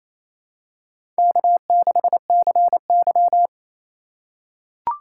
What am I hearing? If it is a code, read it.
Morse code, heard as a keyed tone, about 27 wpm, 700 Hz. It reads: K6CY
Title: USA Call Signs - 4 Characters - 4x Character Spacing - 28wpm